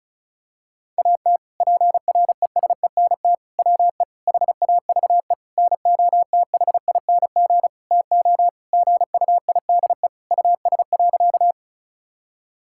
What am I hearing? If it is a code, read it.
Morse code, heard as a keyed tone, about 36 wpm, 700 Hz. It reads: AT PRESENT WE HAVE NOTHING TO GUIDE US.
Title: Sentences from Top 1000 Words - Review Part 5 - 35wpm